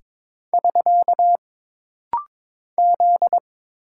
Morse code, heard as a keyed tone, about 22 wpm, 700 Hz. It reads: <SK> Z
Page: This rapid-fire timing is geared to push your ICR (Instant Character Recognition) skills.